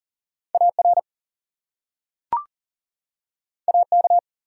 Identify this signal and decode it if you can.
Morse code, heard as a keyed tone, about 40 wpm, 700 Hz. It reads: AR AK
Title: US States and Canadian Provinces 40wpm